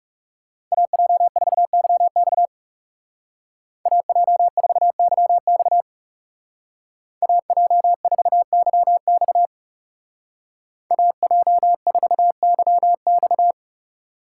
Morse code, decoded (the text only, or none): AJ4YX AJ4YX AJ4YX AJ4YX